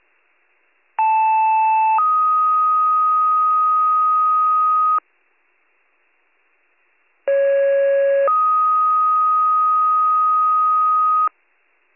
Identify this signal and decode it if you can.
Two-tone page, heard as a one-second tone, then a three-second tone, about 880 then 1250 Hz, about 550 then 1250 Hz. It is heard from 1.0 to 11.3 s.